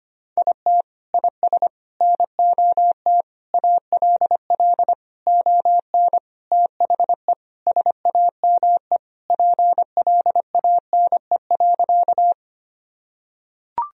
Morse code, heard as a keyed tone, about 25 wpm, 700 Hz. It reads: IT IS NOT ALL ON THE SAME PLANE.
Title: Sentences from Top 1000 Words - Review Part 3 - 25wpm